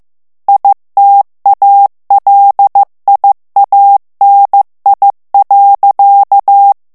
Morse code, heard as a keyed tone, about 15 wpm, 800 Hz. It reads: ITALIANI.
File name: ITALIEN texte clair 1 vitesse 14 mots minute